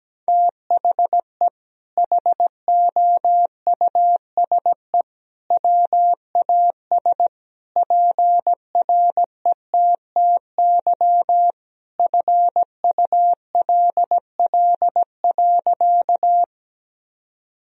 Morse code, heard as a keyed tone, about 17 wpm, 700 Hz. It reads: THE HOUSE WAS PRETTY FULL.
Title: Sentences from Top 1000 Words - Review Part 3 - 17wpm